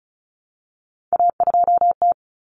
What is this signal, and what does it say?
Morse code, heard as a keyed tone, about 34 wpm, 700 Hz. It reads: A2T
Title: Common Cut Numbers - 3-Digit 35wpm